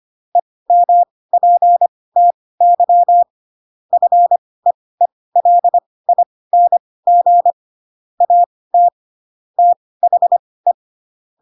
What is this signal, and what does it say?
Morse code, heard as a keyed tone, about 25 wpm, spaced as for 18 wpm, 700 Hz. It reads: EMPTY FEELING AT THE
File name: Pinocchio / Pinocchio ch5 12 25 ik1hge